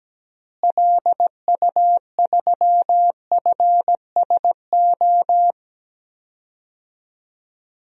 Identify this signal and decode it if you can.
Morse code, heard as a keyed tone, about 17 wpm, 700 Hz. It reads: LU3FSO
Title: DXCC-100 Call Signs - 6 Characters - Repeated 2x - 17wpm